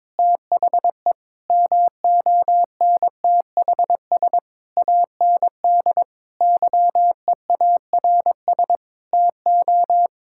Morse code, heard as a keyed tone, about 22 wpm, 700 Hz. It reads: THE MONTHS AND YEARS TO
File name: Sentences with 7-Word Phrases - 22wpm